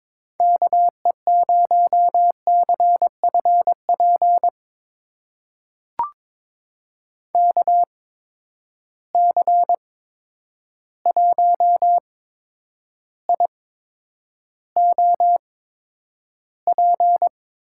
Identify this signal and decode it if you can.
Morse code, heard as a keyed tone, about 22 wpm, spaced as for 9 wpm, 700 Hz. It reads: KE0CFP KC1IOP
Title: USA Call Signs - 6 Characters - 8x Character Spacing - 22wpm